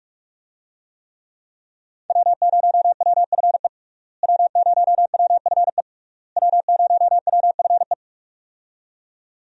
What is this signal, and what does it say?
Morse code, heard as a keyed tone, about 42 wpm, 700 Hz. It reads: W0WFE W0WFE W0WFE
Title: USA Call Signs - 5 Characters - Repeated 3x - 45wpm